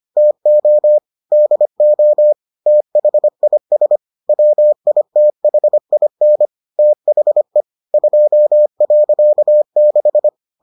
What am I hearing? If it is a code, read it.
Morse code, heard as a keyed tone, about 25 wpm, 600 Hz. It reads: TO DO THIS WITHIN THE 2.6